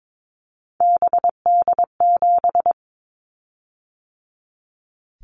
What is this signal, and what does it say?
Morse code, heard as a keyed tone, about 22 wpm, 700 Hz. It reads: BD7
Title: Callsign Prefixes for Top 100 DXCC - 22wpm